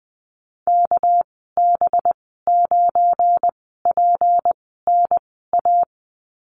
Morse code, heard as a keyed tone, about 20 wpm, 700 Hz. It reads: K B 9 P N A
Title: 238 Lesson 38g - Letter q - Call Signs 20wpm